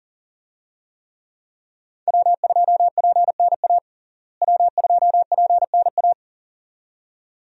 Morse code, heard as a keyed tone, about 40 wpm, 700 Hz. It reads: W2PNA W2PNA
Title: USA Call Signs - 5 Characters - Repeated 2x - 40wpm